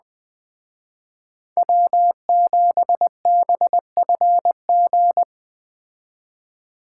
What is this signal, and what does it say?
Morse code, heard as a keyed tone, about 20 wpm, 700 Hz. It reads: W7BFG